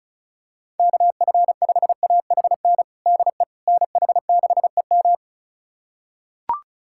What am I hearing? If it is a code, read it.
Morse code, heard as a keyed tone, about 34 wpm, 700 Hz. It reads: KF5AHN DE NH6EM
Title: USA Callsign DE Callsign 35wpm